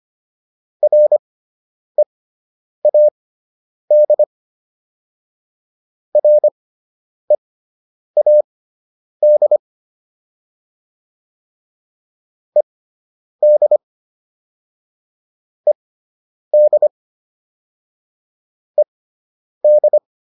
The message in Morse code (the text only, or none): READ READ ED ED ED